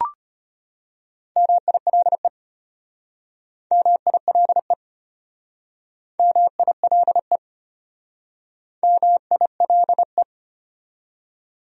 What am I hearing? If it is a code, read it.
Morse code, heard as a keyed tone, about 32 wpm, 700 Hz. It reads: MILE MILE MILE MILE